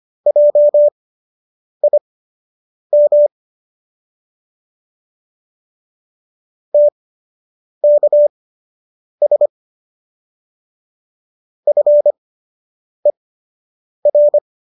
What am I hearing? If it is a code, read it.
Morse code, heard as a keyed tone, about 25 wpm, spaced as for 8 wpm, 590 Hz. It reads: JIM TKS FER